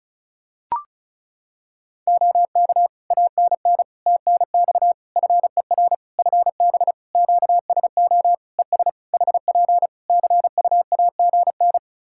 Morse code, heard as a keyed tone, about 36 wpm, 700 Hz. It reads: OK ANN TNX FER FB QSO ES HP CUAGN
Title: Practice QSO - Exchange 4 - Ending - 35wpm